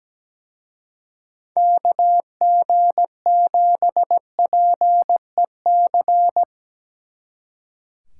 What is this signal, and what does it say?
Morse code, heard as a keyed tone, about 19 wpm, 700 Hz. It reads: KG7PEC
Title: Speed Racing - USA Call Signs 17wpm